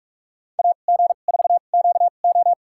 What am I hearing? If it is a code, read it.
Morse code, heard as a keyed tone, about 46 wpm, spaced as for 32 wpm, 700 Hz. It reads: AG4QO
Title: USA Call Signs - 2x Character Spacing - 45wpm